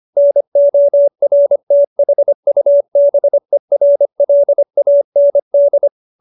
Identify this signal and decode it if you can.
Morse code, heard as a keyed tone, about 25 wpm, 570 Hz. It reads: NORTHUBERLAND
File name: QSO315 25